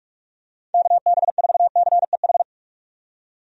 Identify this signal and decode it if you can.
Morse code, heard as a keyed tone, about 46 wpm, 700 Hz. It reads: KB4CEH
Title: USA Call Signs - 6 Characters - 45wpm